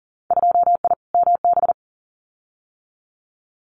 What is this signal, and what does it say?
Morse code, heard as a keyed tone, about 39 wpm, 700 Hz. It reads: 2I MB
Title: Winter Field Day Exchanges Encore 40wpm